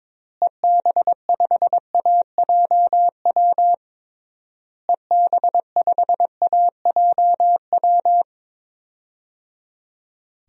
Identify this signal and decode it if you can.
Morse code, heard as a keyed tone, about 22 wpm, 700 Hz. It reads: EB5AJW EB5AJW
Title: DXCC-100 Call Signs - 6 Characters - Repeated 3x - 22wpm